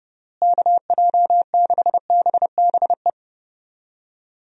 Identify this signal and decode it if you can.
Morse code, heard as a keyed tone, about 30 wpm, 700 Hz. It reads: KJ6BBE
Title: USA Call Signs - 6 Characters - Repeated 3x - 30wpm